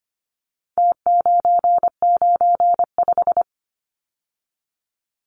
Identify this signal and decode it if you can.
Morse code, heard as a keyed tone, about 25 wpm, 700 Hz. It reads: T995
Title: Numbers - Contest Style - 4-Digit 25wpm